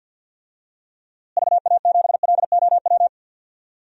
Morse code, heard as a keyed tone, about 52 wpm, 700 Hz. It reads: VA7LOW
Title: DXCC-100 Call Signs - 6 Characters - Repeated 3x - 50wpm